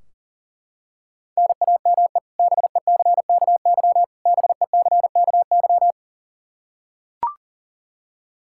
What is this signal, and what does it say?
Morse code, heard as a keyed tone, about 38 wpm, 700 Hz. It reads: NAME BECKY BECKY
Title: K1USN SST QSO Practice 40wpm